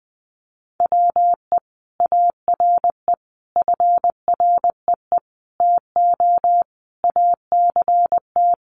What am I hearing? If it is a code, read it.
Morse code, heard as a keyed tone, about 20 wpm, 700 Hz. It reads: WE ARE FREE TO ACT
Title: Sentences from Top 500 Words 20wpm